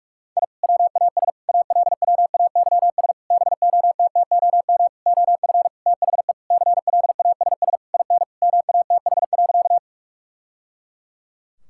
Morse code, heard as a keyed tone, about 45 wpm, 700 Hz. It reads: I WAS ALWAYS BOTTOM OF THE CLASS IN MATH.
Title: Sentences from Top 1000 Words - Review Part 6 - 45wpm